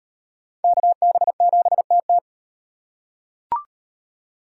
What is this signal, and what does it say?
Morse code, heard as a keyed tone, about 38 wpm, 700 Hz. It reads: KB7TT